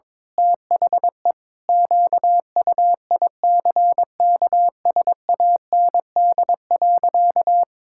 Morse code, heard as a keyed tone, about 22 wpm, 700 Hz. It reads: THE QUICKSAND.
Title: Pangram Challenge v2 - 22wpm